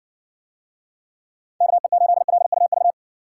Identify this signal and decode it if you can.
Morse code, heard as a keyed tone, about 60 wpm, 700 Hz. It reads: KE8LUV